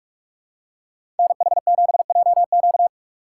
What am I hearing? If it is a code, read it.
Morse code, heard as a keyed tone, about 46 wpm, 690 Hz. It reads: NH7JQ
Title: USA Call Signs - 5 Characters - 2x Character Spacing - 45wpm